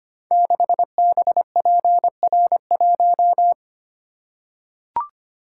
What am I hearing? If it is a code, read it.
Morse code, heard as a keyed tone, about 25 wpm, 700 Hz. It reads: 6BPR1
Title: Code Groups - 5 Letter:Numbers - 25wpm